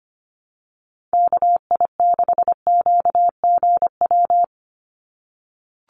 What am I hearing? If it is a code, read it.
Morse code, heard as a keyed tone, about 25 wpm, 700 Hz. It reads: KI6QGW
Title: USA Call Signs 25wpm